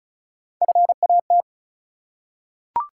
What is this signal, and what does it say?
Morse code, heard as a keyed tone, about 36 wpm, 700 Hz. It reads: FAT
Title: Three Letter Words - 35wpm